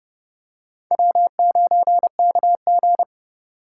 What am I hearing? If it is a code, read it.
Morse code, heard as a keyed tone, about 30 wpm, 700 Hz. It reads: W9KG